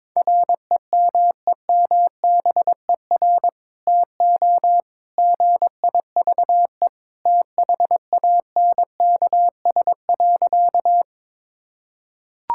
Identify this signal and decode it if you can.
Morse code, heard as a keyed tone, about 22 wpm, 700 Hz. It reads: REMEMBER TO GIVE THANKS.
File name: practice / 22wpm / Sentences with 2-Word Phrases - 22wpm